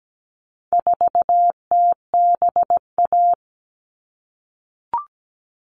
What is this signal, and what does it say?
Morse code, heard as a keyed tone, about 17 wpm, 700 Hz. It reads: 4TBA
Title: All Cut Numbers - 4-Digit 17wpm